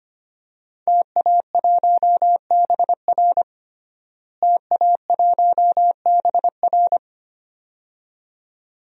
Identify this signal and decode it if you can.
Morse code, heard as a keyed tone, about 25 wpm, 700 Hz. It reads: TA1BR TA1BR
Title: DXCC-100 Call Signs - 5 Characters - Repeated 2x - 25wpm